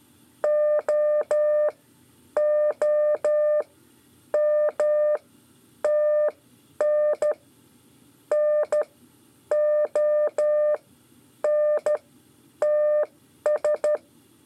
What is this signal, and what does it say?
Morse code, heard as a keyed tone, about 13 wpm, spaced as for 9 wpm, 600 Hz. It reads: OOMTN NONTS
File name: 5 LEZIONE RIPASSO 1